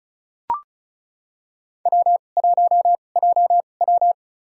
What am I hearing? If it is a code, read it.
Morse code, heard as a keyed tone, about 36 wpm, spaced as for 25 wpm, 700 Hz. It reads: W1JW
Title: USA Call Signs - 4 Characters - 2x Character Spacing - 35wpm